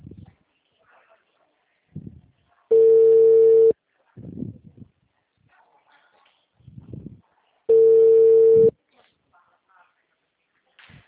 Ringback tone (China, 450 Hz).
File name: out- -2012